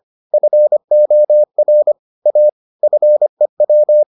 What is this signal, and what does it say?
Morse code, heard as a keyed tone, about 25 wpm, 600 Hz. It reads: FOR A FEW